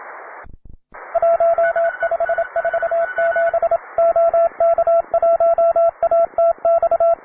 Morse code, heard as a keyed tone, about 27 wpm, 670 Hz, with radio noise. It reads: 1547 OK1ATX
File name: wae / cw beginner / py2rh cw